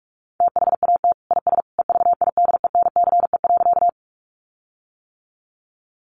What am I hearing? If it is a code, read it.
Morse code, heard as a keyed tone, about 44 wpm, 700 Hz. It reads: THAT IS EVIDENCE.
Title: Sentences from Top 1300 Words 45wpm